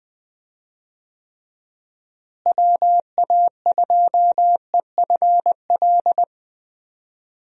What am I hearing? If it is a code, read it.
Morse code, heard as a keyed tone, about 21 wpm, 700 Hz. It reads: WA2EFL
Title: USA Call Signs - 6 Characters - Repeated 3x - 20wpm